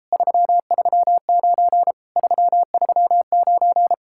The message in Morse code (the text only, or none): 339 339